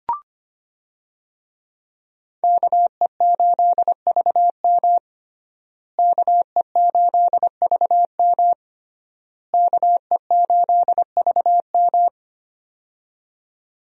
Morse code, heard as a keyed tone, about 25 wpm, 700 Hz. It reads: KE8VM KE8VM KE8VM
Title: USA Call Signs - 5 Characters - Repeated 3x - 25wpm